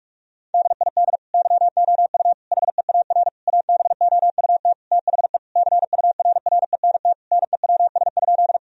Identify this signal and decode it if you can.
Morse code, heard as a keyed tone, about 45 wpm, 700 Hz. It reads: DID YOU HEAR ABOUT THE CURRENT NEWS?
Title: Sentences from Top 1000 Words - Review Part 4 - 45wpm